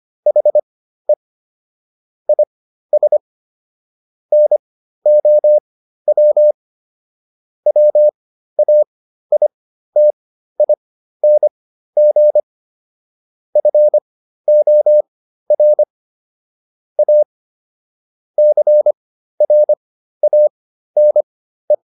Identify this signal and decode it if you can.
Morse code, heard as a keyed tone, about 25 wpm, spaced as for 13 wpm, 600 Hz. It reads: HE IS NOW WAITING FOR A CRANE